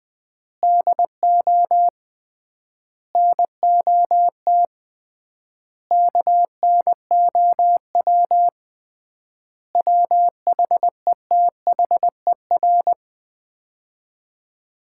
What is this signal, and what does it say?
Morse code, heard as a keyed tone, about 20 wpm, 700 Hz. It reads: DO NOT KNOW WHETHER
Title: 3x Word-Spacing - Sets of 5 Words - 20wpm